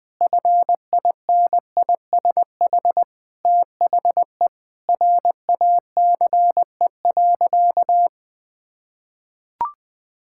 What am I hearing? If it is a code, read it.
Morse code, heard as a keyed tone, about 21 wpm, 700 Hz. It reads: FINISH THE RACE.
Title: Sentences with 8-Word Phrases - 20wpm